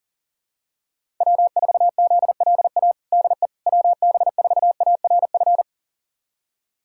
Morse code, heard as a keyed tone, about 37 wpm, 700 Hz. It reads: W4ZLA DE WB4ARF
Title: USA Callsign DE Callsign 40wpm